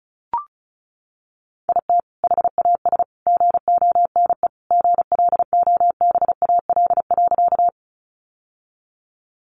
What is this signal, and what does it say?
Morse code, heard as a keyed tone, about 34 wpm, 700 Hz. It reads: IT HAS GONE GLOBAL.
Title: Sentences from Top 1200 Words 35wpm